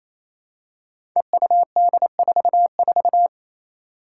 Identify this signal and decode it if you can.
Morse code, heard as a keyed tone, about 27 wpm, 700 Hz. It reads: EUD44
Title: Five Challenging Letter-Number Combinations 28wpm